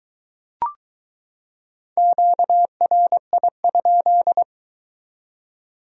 Morse code, heard as a keyed tone, about 22 wpm, 700 Hz. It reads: QRI?